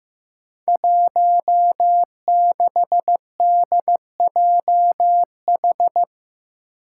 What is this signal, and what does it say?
Morse code, heard as a keyed tone, about 15 wpm, 700 Hz. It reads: 16DJH
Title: Five Challenging Letter-Number Combinations 15wpm